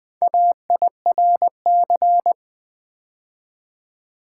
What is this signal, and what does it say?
Morse code, heard as a keyed tone, about 20 wpm, 700 Hz. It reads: AIRC
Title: Build A Word - 2 Letters At A Time - 8-Letter Words - 20wpm